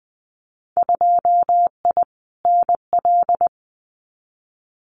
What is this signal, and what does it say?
Morse code, heard as a keyed tone, about 20 wpm, 700 Hz. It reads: 2I NL
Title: Winter Field Day Exchanges Encore 20wpm